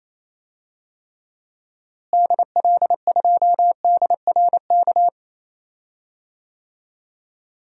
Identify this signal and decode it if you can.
Morse code, heard as a keyed tone, about 28 wpm, 700 Hz. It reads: DL2DRK
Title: DXCC-100 Call Signs - 6 Characters - Repeated 2x - 28wpm